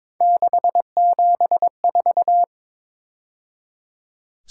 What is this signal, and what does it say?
Morse code, heard as a keyed tone, about 26 wpm, 700 Hz. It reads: 674
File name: Speed-Racing - Three Letter-Number Combinations - 22wpm